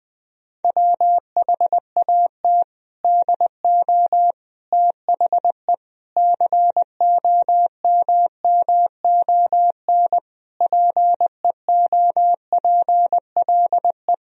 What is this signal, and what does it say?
Morse code, heard as a keyed tone, about 20 wpm, 700 Hz. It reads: WHAT DO THE COMMON PEOPLE
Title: Sentences from Top 1000 Words - Review Part 3 - 20wpm